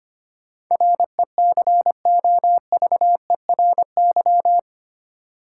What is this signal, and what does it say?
Morse code, heard as a keyed tone, about 25 wpm, 700 Hz. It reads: RECOVERY
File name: 8 Letter Words - First 3 Letters Given - 25wpm